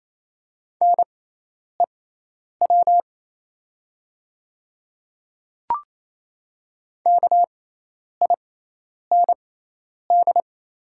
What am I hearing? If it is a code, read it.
Morse code, heard as a keyed tone, about 27 wpm, spaced as for 10 wpm, 700 Hz. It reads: NEW KIND
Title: 6x Character Spacing - Top 200 Words - 28wpm